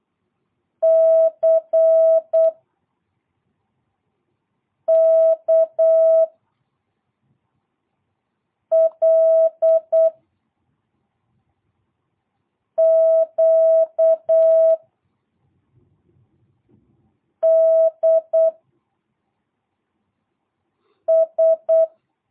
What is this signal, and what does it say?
Morse code, heard as a keyed tone, about 8 wpm, 660 Hz. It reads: C K L Q D S
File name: jh8cbh / 2022cw